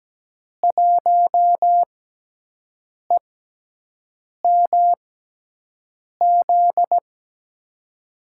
Morse code, heard as a keyed tone, about 17 wpm, spaced as for 6 wpm, 700 Hz. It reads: 1EMZ
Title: USA Call Signs - 6 Characters - 6x Character Spacing - 17wpm